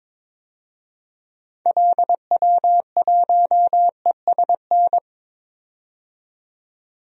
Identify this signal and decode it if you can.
Morse code, heard as a keyed tone, about 21 wpm, 700 Hz. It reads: LW1ESN
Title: DXCC-100 Call Signs - 6 Characters - 22wpm